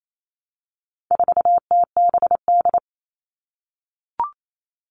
Morse code, heard as a keyed tone, about 28 wpm, 700 Hz. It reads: 4TBD
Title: All Cut Numbers - 4-Digit 28wpm